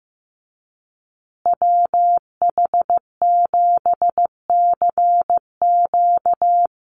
Morse code, heard as a keyed tone, about 15 wpm, 700 Hz. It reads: WH7CQ
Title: USA Call Signs Repeated 2x 15wpm